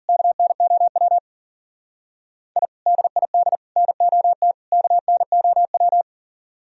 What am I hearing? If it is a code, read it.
Morse code, heard as a keyed tone, about 48 wpm, 700 Hz. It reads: KNOW I DID NOT KNOW
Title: Speed-Racing - Sets of 4 Words 40wpm